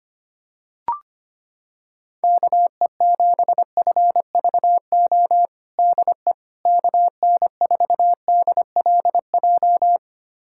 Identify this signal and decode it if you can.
Morse code, heard as a keyed tone, about 25 wpm, 700 Hz. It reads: KE7FVO DE KN4DLJ
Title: USA Callsign DE Callsign - 6 Characters - 25wpm